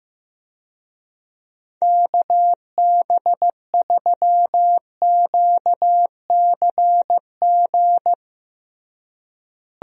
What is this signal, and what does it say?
Morse code, heard as a keyed tone, about 15 wpm, 700 Hz. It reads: KB3QCG
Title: USA Call Signs - 6 Characters - Repeated 3x - 15wpm